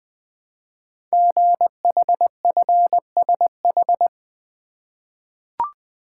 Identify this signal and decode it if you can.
Morse code, heard as a keyed tone, about 20 wpm, 700 Hz. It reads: GHFSH
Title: Code Groups - 5 Letters - 20wpm